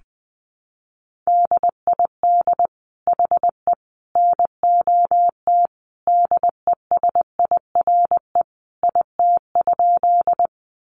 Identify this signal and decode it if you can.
Morse code, heard as a keyed tone, about 20 wpm, 700 Hz. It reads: DID HE NOT DESIRE IT?
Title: Sentences from Top 1100 Words 20wpm